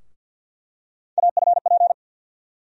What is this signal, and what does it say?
Morse code, heard as a keyed tone, about 49 wpm, 700 Hz. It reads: AFP
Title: USA Call Sign Postfixes - 3 Letters 50wpm